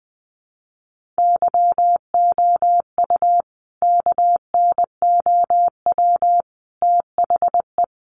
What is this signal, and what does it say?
Morse code, heard as a keyed tone, about 19 wpm, 700 Hz. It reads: YOU KNOW THE
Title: Sentences from Top 1500 Words 20wpm